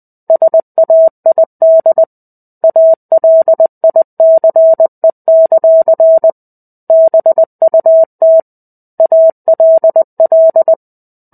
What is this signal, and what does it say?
Morse code, heard as a keyed tone, about 20 wpm, 650 Hz. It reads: SAID ALICE; BUT ALL
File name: 05-Chapter-05 20 WPM